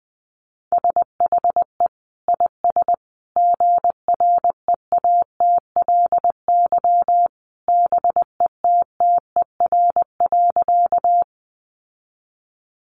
Morse code, heard as a keyed tone, about 20 wpm, 700 Hz. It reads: SHE IS GREATLY BETTER.
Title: Sentences from Top 1400 Words 20wpm